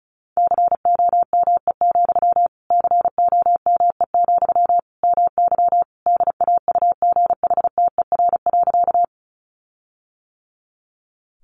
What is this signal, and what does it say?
Morse code, heard as a keyed tone, about 34 wpm, 700 Hz. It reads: COME, COME, MY DAUGHTER.
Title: Sentences from Top 1500 Words 35wpm